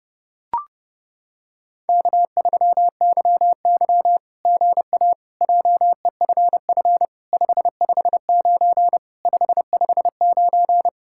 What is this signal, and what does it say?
Morse code, heard as a keyed tone, about 31 wpm, 700 Hz. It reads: K3YY GA JEFF 559 559
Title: SOTA QSO Practice 30wpm